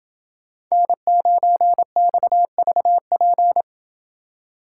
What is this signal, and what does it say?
Morse code, heard as a keyed tone, about 28 wpm, 700 Hz. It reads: N9XVP